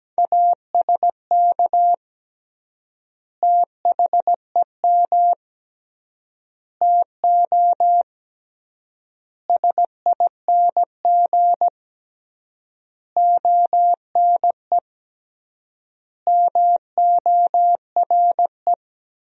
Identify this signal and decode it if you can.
Morse code, heard as a keyed tone, about 17 wpm, 700 Hz. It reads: ASK THEM TO SING ONE MORE
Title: 3x Word-Spacing - Sentences from Top 400 Words - 17wpm